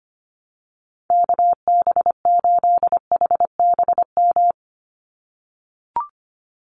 Morse code, heard as a keyed tone, about 25 wpm, 700 Hz. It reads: KB8HBM
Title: USA Call Signs 25wpm